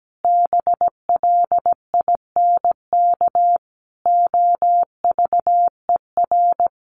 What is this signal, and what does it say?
Morse code, heard as a keyed tone, about 17 wpm, 700 Hz. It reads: BLINK OVER
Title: A Sound Of Thunder 17wpm